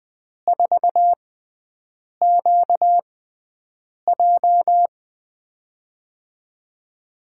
Morse code, heard as a keyed tone, about 20 wpm, spaced as for 7 wpm, 700 Hz. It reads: 4QJ
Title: USA Call Signs - 5 Characters - 6x Character Spacing - 20wpm